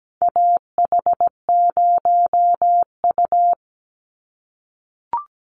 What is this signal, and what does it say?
Morse code, heard as a keyed tone, about 17 wpm, 700 Hz. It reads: AH0U
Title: Call Signs - 17wpm